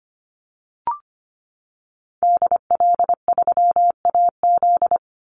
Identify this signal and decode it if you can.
Morse code, heard as a keyed tone, about 25 wpm, 700 Hz. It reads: DL3AZ
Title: Call Signs - 25wpm